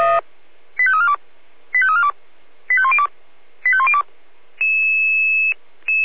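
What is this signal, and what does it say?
Five-tone selective call (ZVEI), heard as quick runs of five tones, tones of about 80 ms. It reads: 85342 85342 85102 85102
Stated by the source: ZVEI five-tone selective call 85832 received with an SDR.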